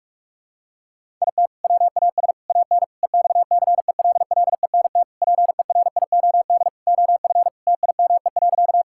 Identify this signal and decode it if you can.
Morse code, heard as a keyed tone, about 46 wpm, 700 Hz. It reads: IT WAS AN EXCELLENT PERIOD OF TIME.
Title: Sentences from Top 1000 Words - Review Part 5 - 45wpm